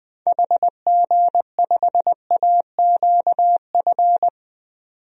Morse code, heard as a keyed tone, about 21 wpm, 700 Hz. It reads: HG5AQF
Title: DXCC-100 Call Signs - 6 Characters - Repeated 3x - 20wpm